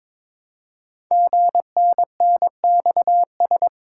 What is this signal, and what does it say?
Morse code, heard as a keyed tone, about 22 wpm, 700 Hz. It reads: GNNXS
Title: Code Groups - 5 Letters - 22wpm